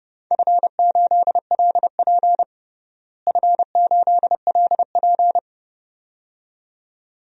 Morse code, heard as a keyed tone, about 30 wpm, 700 Hz. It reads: F8LP F8LP
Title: DXCC-100 Call Signs - 4 Characters - Repeated 2x - 30wpm